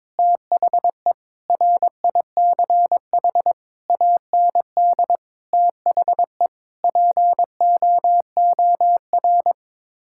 Morse code, heard as a keyed tone, about 21 wpm, 700 Hz. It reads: THE RICH AND THE POOR
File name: practice / 22wpm / Sets of 8 Words - 22wpm